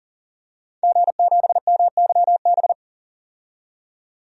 Morse code, heard as a keyed tone, about 38 wpm, 700 Hz. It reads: G7MYB
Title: DXCC-100 Call Signs - 5 Characters - 40wpm